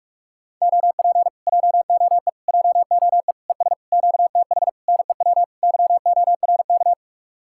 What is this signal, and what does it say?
Morse code, heard as a keyed tone, about 49 wpm, 700 Hz. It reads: OP JOE JOE ES QTH NEW YORK